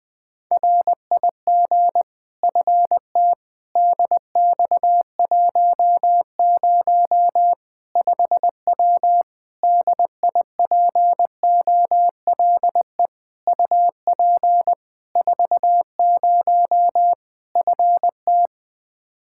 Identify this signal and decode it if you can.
Morse code, heard as a keyed tone, about 21 wpm, 700 Hz. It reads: RIG FT DX10 5W DIPOLE UP 40 FT
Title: Practice QSO - Exchange 2 - Info - 20wpm